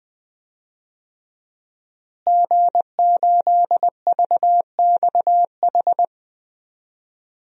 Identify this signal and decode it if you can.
Morse code, heard as a keyed tone, about 20 wpm, 700 Hz. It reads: G8VXH
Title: DXCC-100 Call Signs - 5 Characters - Repeated 2x - 20wpm